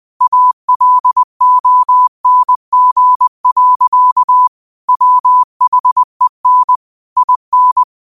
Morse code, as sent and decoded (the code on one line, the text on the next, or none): .- .-.. --- -. --. .-.-.- / .-- .... -. / .. -.
ALONG. WHEN IN